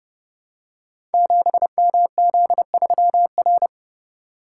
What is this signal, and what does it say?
Morse code, heard as a keyed tone, about 30 wpm, 700 Hz. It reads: 7MZ3R
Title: Code Groups - 5 Letter:Numbers - 30wpm